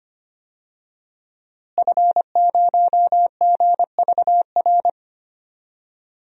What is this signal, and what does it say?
Morse code, heard as a keyed tone, about 25 wpm, 700 Hz. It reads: F0GVR
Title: DXCC-100 Call Signs - 5 Characters - Repeated 3x - 25wpm